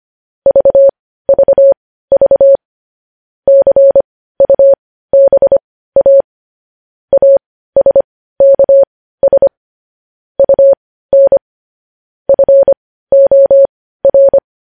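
Morse code, heard as a keyed tone, about 25 wpm, spaced as for 15 wpm, 550 Hz. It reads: VVV CUBA ASKS UN FOR
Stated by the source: Morse code transcription of news headlines.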